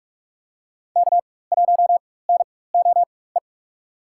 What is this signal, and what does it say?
Morse code, heard as a keyed tone, about 44 wpm, spaced as for 21 wpm, 690 Hz. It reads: K1NOE
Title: USA Call Signs - 4x Character Spacing - 45wpm